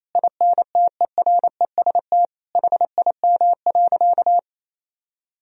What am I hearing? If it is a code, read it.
Morse code, heard as a keyed tone, about 28 wpm, 700 Hz. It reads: INTEREST HIM.
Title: Sentences from Top 1000 Words - Review Part 2 - 28wpm